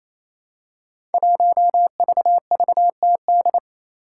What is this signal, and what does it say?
Morse code, heard as a keyed tone, about 29 wpm, 700 Hz. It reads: 1VVTD
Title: Five Challenging Letter-Number Combinations 28wpm